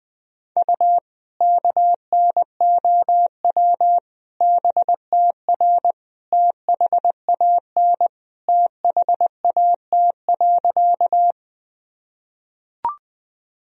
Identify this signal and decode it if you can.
Morse code, heard as a keyed tone, about 20 wpm, 700 Hz. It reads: U KNOW BTR THAN THAT.
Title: Sentences from Top 400 Words and Abbreviations 20wpm